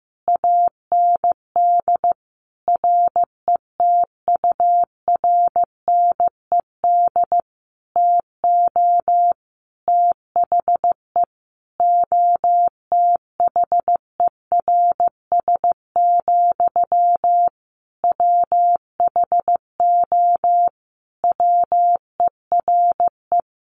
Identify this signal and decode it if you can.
Morse code, heard as a keyed tone, about 15 wpm, 700 Hz. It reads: AND RETURNED TO THE OTHERS, WHO WERE
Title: A Sound Of Thunder 15wpm